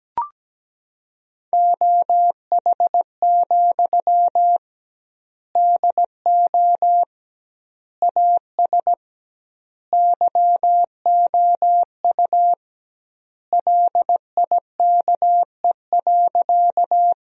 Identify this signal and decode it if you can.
Morse code, heard as a keyed tone, about 17 wpm, 700 Hz. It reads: OH, DO AS YOU LIKE.
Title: 2x Word-Spacing - Sentences from Top 500 Words - 17wpm